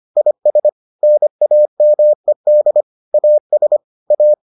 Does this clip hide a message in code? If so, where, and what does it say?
Yes: Morse code from 0.2 to 4.3 s, heard as a keyed tone, about 25 wpm, 600 Hz. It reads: IS NAMED AS A